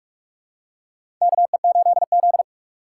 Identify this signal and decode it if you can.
Morse code, heard as a keyed tone, about 46 wpm, 700 Hz. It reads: KE8Z
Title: USA Call Signs - 3x Character Spacing - 45wpm